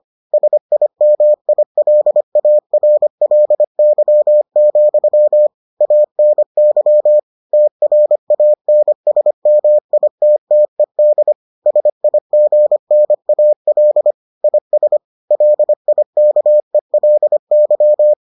Morse code, heard as a keyed tone, about 25 wpm, 600 Hz. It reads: SIMILARLY, ANY TRANSMITTED SIGNAL IS LIKELY